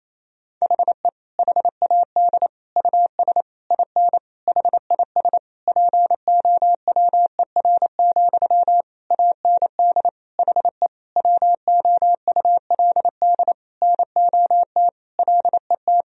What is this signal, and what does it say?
Morse code, heard as a keyed tone, about 27 wpm, 700 Hz. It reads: HE HAD US IN HIS POWER, AND HE WOULD NOT LET
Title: Sentences from Top 1000 Words - Review Part 2 - 28wpm